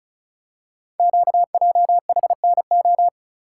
Morse code, heard as a keyed tone, about 36 wpm, 700 Hz. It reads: QJHNO
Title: Code Groups - 5 Letters - 35wpm